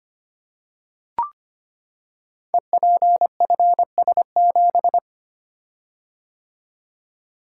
Morse code, heard as a keyed tone, about 25 wpm, 700 Hz. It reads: EPFS7
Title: Code Groups - 5 Letter:Numbers - 25wpm